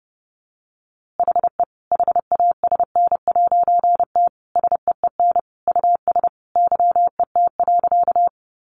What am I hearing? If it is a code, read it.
Morse code, heard as a keyed tone, about 30 wpm, 700 Hz. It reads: HE HASN'T SEEN US YET.
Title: A Sound Of Thunder 30wpm